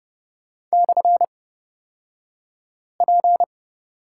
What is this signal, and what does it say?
Morse code, heard as a keyed tone, about 30 wpm, spaced as for 5 wpm, 690 Hz. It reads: /P
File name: Lesson 36d - BT - Two Characters 30wpm